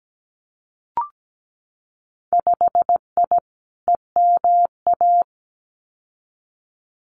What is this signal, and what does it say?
Morse code, heard as a keyed tone, about 17 wpm, 700 Hz. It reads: 5I EMA
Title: Winter Field Day Exchanges Encore 17wpm